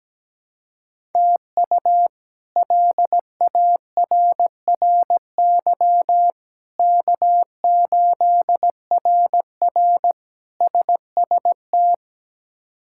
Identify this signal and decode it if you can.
Morse code, heard as a keyed tone, about 17 wpm, 700 Hz. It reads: TU LARRY K8RR SST
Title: K1USN SST QSO Practice - 17wpm